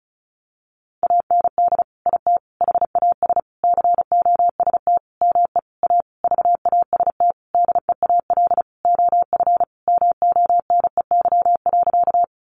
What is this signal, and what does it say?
Morse code, heard as a keyed tone, about 34 wpm, 700 Hz. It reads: AND IT HAS COST ME A VAST DEAL OF MONEY.
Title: Sentences from Top 1300 Words 35wpm